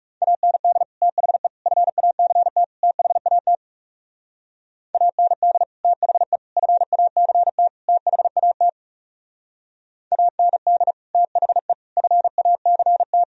Morse code, heard as a keyed tone, about 38 wpm, 700 Hz. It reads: AND THE FACT THAT AND THE FACT THAT AND THE FACT
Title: Speed-Racing - Sets of 4 Words 30wpm